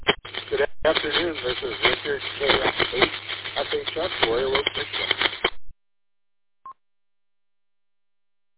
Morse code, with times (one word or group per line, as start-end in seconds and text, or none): none